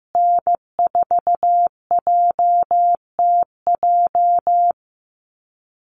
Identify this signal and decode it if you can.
Morse code, heard as a keyed tone, about 15 wpm, 700 Hz. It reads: N4JTJ
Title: USA Call Signs Repeated 3x 15wpm